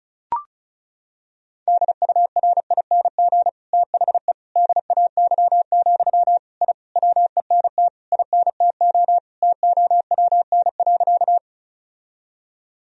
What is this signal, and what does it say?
Morse code, heard as a keyed tone, about 36 wpm, 700 Hz. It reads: DURING THE DAY, I WENT INTO TOWN.
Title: Sentences from Top 1000 Words - Review Part 2 - 35wpm